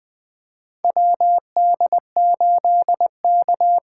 Morse code, heard as a keyed tone, about 20 wpm, 700 Hz. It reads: WD8K
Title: USA Call Signs - 6 Characters - 8x Character Spacing - 20wpm